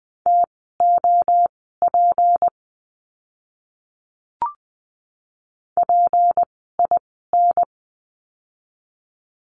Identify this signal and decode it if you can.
Morse code, heard as a keyed tone, about 20 wpm, spaced as for 14 wpm, 700 Hz. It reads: TOP PIN
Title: 153 Lesson 25f - Letter p - Words 20wpm